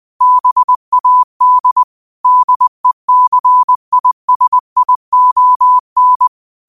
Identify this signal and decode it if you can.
Morse code, heard as a keyed tone, about 20 wpm, 1000 Hz. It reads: BAD DECISION
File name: qotdfiles / 20wpm